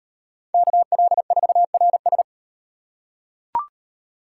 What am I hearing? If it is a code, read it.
Morse code, heard as a keyed tone, about 38 wpm, 700 Hz. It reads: KL4RS